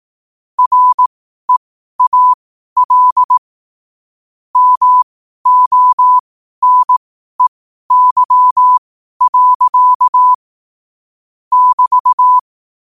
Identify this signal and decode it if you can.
Morse code, heard as a keyed tone, about 18 wpm, spaced as for 12 wpm, 1000 Hz. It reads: REAL MONEY. <BT>
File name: qotdfiles / 12wpm